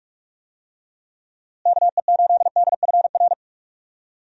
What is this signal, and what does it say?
Morse code, heard as a keyed tone, about 43 wpm, 700 Hz. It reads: KE8DFR
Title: USA Call Signs - 6 Characters - Repeated 3x - 45wpm